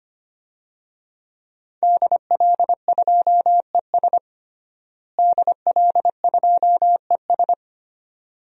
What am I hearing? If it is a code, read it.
Morse code, heard as a keyed tone, about 25 wpm, 700 Hz. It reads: DL2ES DL2ES
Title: DXCC-100 Call Signs - 5 Characters - Repeated 2x - 25wpm